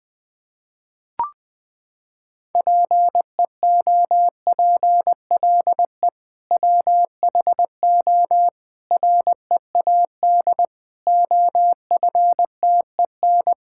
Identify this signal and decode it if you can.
Morse code, heard as a keyed tone, about 20 wpm, 700 Hz. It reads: PEOPLE WHO READ OFTEN
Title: Sentences with 2-Word Phrases - 20wpm